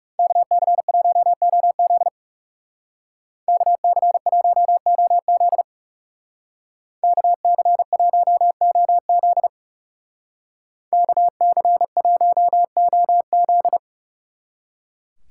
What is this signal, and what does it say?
Morse code, heard as a keyed tone, about 37 wpm, 700 Hz. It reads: KC1OZ KC1OZ KC1OZ KC1OZ
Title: Speed Racing - USA 2x2 Call Signs 30wpm